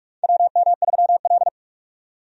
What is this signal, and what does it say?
Morse code, heard as a keyed tone, about 44 wpm, 700 Hz. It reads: WM3L
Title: USA Call Signs - 4 Characters - 45wpm